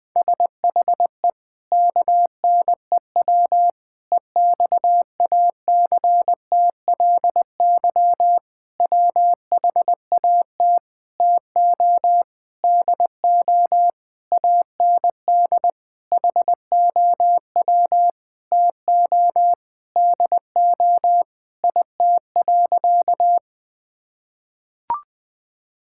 Morse code, heard as a keyed tone, about 20 wpm, 700 Hz. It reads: SHE KNEW EXACTLY WHAT TO DO AND HOW TO DO IT.
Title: Sentences with 8-Word Phrases - 20wpm